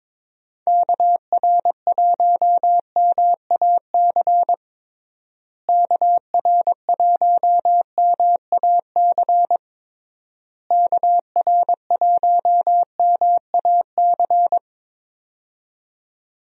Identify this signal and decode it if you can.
Morse code, heard as a keyed tone, about 22 wpm, 700 Hz. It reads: KR1MAC KR1MAC KR1MAC
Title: USA Call Signs - 6 Characters - Repeated 3x - 22wpm